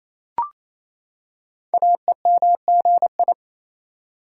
Morse code, heard as a keyed tone, about 27 wpm, 700 Hz. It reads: AEMGI
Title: Code Groups - 5 Letters - 28wpm